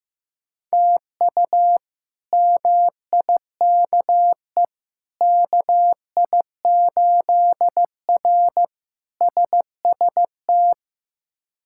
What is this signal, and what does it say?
Morse code, heard as a keyed tone, about 15 wpm, 700 Hz. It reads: TU MIKE KI8R SST
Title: K1USN SST QSO Practice - 15wpm